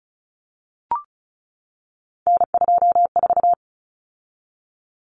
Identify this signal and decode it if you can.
Morse code, heard as a keyed tone, about 34 wpm, 700 Hz. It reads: N24
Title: Common Cut Numbers - 3-Digit 35wpm